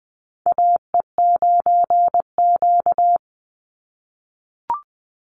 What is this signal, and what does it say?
Morse code, heard as a keyed tone, about 20 wpm, 700 Hz. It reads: AE9Q
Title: NA SOTA Chaser Callsigns 20wpm